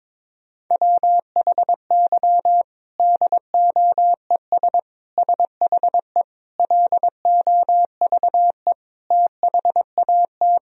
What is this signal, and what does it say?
Morse code, heard as a keyed tone, about 22 wpm, 700 Hz. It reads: WHY DOES SHE LOVE THAT
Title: Sentences from Top 1000 Words - Review Part 4 - 22wpm